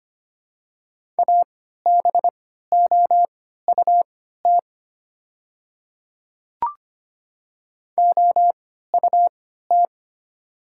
Morse code, heard as a keyed tone, about 25 wpm, spaced as for 14 wpm, 700 Hz. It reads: ABOUT OUT
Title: 3x Character Spacing - Top 100 Words - 25wpm